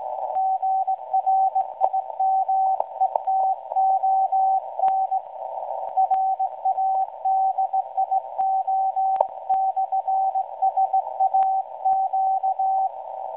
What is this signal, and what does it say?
Morse code, heard as a keyed tone, about 17 wpm, 760 Hz, with radio noise. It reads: GRIMETON RADIO/SAQ
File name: WebSDR records / SAQ at PI4THT msg 0800-0804 17.2kHz